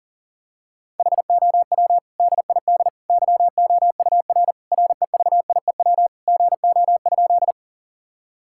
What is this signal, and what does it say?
Morse code, heard as a keyed tone, about 40 wpm, 700 Hz. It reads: HOW DID YOUR REVIEW GO?
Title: Sentences from Top 1000 Words - Review Part 6 - 40wpm